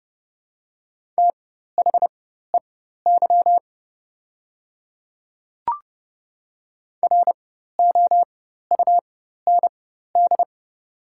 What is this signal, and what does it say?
Morse code, heard as a keyed tone, about 29 wpm, spaced as for 14 wpm, 700 Hz. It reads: THEY ROUND
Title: 4x Character Spacing - Top 200 Words - 30wpm